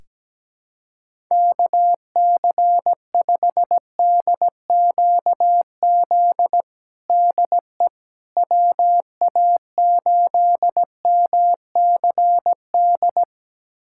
Morse code, heard as a keyed tone, about 17 wpm, 700 Hz. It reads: KC5DQZ DE WA8MCD
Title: USA Callsign DE Callsign 17wpm